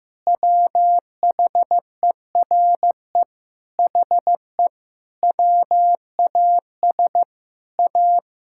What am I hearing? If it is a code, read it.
Morse code, heard as a keyed tone, about 15 wpm, 700 Hz. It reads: WHERE HE WAS A
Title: Sentences from Top 1000 Words - Review Part 5 - 15wpm